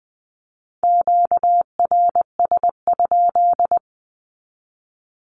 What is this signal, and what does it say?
Morse code, heard as a keyed tone, about 20 wpm, 700 Hz. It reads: QRS?
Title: Q-Codes 20wpm